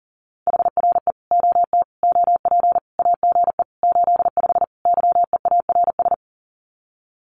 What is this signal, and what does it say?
Morse code, heard as a keyed tone, about 40 wpm, 700 Hz. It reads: HRE OT OP AGE 85 YEARS
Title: Common QSO Phrases 40wpm